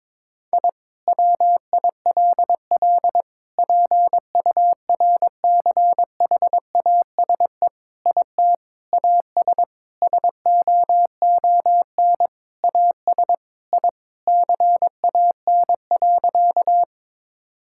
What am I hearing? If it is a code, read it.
Morse code, heard as a keyed tone, about 22 wpm, 700 Hz. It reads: I WILL PURCHASE IT AS SOON AS I CAN.
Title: Sentences from Top 1000 Words - Review Part 5 - 22wpm